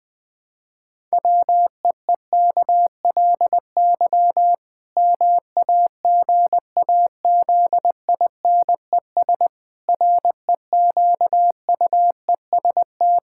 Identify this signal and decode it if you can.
Morse code, heard as a keyed tone, about 20 wpm, 700 Hz. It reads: WEEKLY MAGAZINES REQUEST
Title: Pangram Challenge v2 - 20wpm